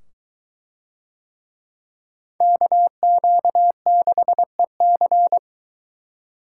Morse code, heard as a keyed tone, about 23 wpm, 700 Hz. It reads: KQ6EC